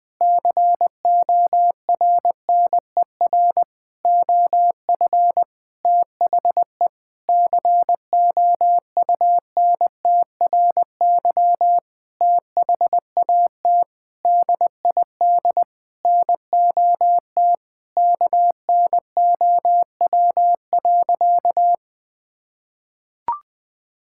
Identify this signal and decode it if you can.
Morse code, heard as a keyed tone, about 20 wpm, 700 Hz. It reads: CORNER OF THE COUNTRY THAT DID NOT KNOW.
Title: Sentences from Top 1600 Words 20wpm